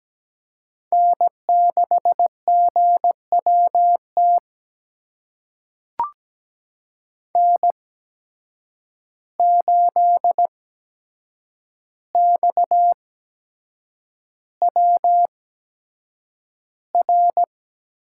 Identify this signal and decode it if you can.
Morse code, heard as a keyed tone, about 17 wpm, spaced as for 7 wpm, 700 Hz. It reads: N6GWT N8XWR
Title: USA Call Signs - 5 Characters - 8x Character Spacing - 17wpm